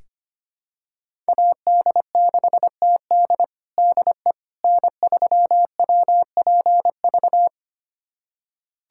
Morse code, heard as a keyed tone, about 25 wpm, 700 Hz. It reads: AD6TD DE N3WPV
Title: USA Callsign DE Callsign - 5 Characters - 25wpm